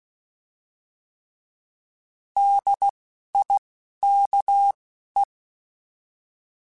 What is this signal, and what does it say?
Morse code, heard as a keyed tone, about 16 wpm, spaced as for 12 wpm, 800 Hz. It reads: DIKE
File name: Alphabetical / Alphabetical mp3s / Ds @10